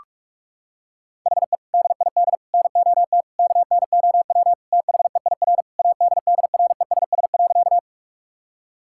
Morse code, heard as a keyed tone, about 41 wpm, 700 Hz. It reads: HE DID NOT KNOW THEIR ADDRESS.
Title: Sentences from Top 1000 Words - Review Part 6 - 45wpm